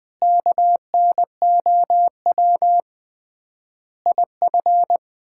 Morse code, heard as a keyed tone, about 20 wpm, 700 Hz. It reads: KNOW IF
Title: 3x Word-Spacing - Sets of 5 Words - 20wpm